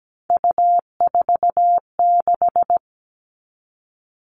Morse code, heard as a keyed tone, about 17 wpm, 700 Hz. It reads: U46
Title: All Cut Numbers - 3-Digit 17wpm